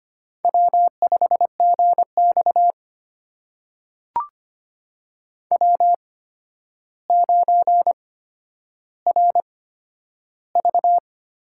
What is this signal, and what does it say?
Morse code, heard as a keyed tone, about 25 wpm, spaced as for 11 wpm, 700 Hz. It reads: W5GX W9RV
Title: USA Call Signs - 8x Character Spacing - 25wpm